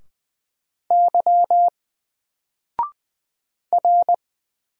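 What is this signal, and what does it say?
Morse code, heard as a keyed tone, about 20 wpm, 700 Hz. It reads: Y R
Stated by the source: Single letter and number Morse drill at 20 wpm, from N4LCD.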